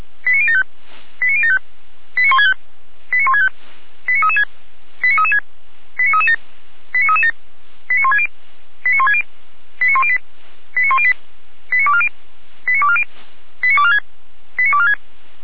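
Five-tone selective call (ZVEI), heard as quick runs of five tones, tones of about 70 ms. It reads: 89075 89075 89165 89165 89336 89336 89337 89337 89160 89160 89108 89108 89240 89240 89246 89246
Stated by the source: Five-tone ZVEI selective call received on an SDR.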